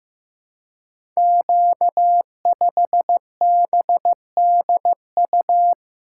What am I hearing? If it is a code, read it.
Morse code, heard as a keyed tone, about 15 wpm, 700 Hz. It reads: Q5BDU
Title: Code Groups - 5 Letter:Numbers - 15wpm